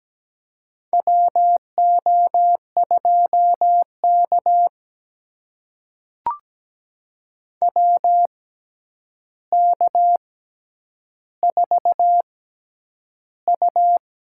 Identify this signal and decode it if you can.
Morse code, heard as a keyed tone, about 17 wpm, spaced as for 9 wpm, 700 Hz. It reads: WO2K WK4U
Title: USA Call Signs - 6x Character Spacing - 17wpm